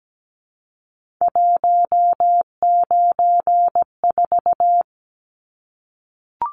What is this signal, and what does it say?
Morse code, heard as a keyed tone, about 17 wpm, 700 Hz. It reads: 194
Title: Numbers - 3-Digit 17wpm